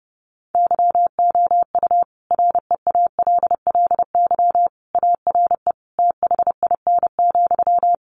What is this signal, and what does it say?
Morse code, heard as a keyed tone, about 30 wpm, 700 Hz. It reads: YOU REALLY ARE THIN,
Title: Sentences from Top 1200 Words 30wpm